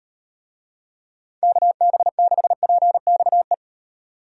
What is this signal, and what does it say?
Morse code, heard as a keyed tone, about 34 wpm, 700 Hz. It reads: KB6PXE